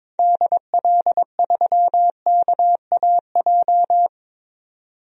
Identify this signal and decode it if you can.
Morse code, heard as a keyed tone, about 22 wpm, 700 Hz. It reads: DL3KAJ
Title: DXCC-100 Call Signs - 6 Characters - 22wpm